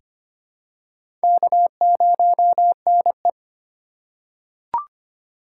Morse code, heard as a keyed tone, about 25 wpm, 700 Hz. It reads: K0NE
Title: USA Call Signs - 4 Characters - 3x Character Spacing - 25wpm